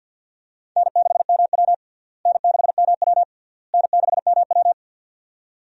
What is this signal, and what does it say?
Morse code, heard as a keyed tone, about 46 wpm, 700 Hz. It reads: N6MW N6MW N6MW
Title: USA Call Signs - 4 Characters - Repeated 3x - 50wpm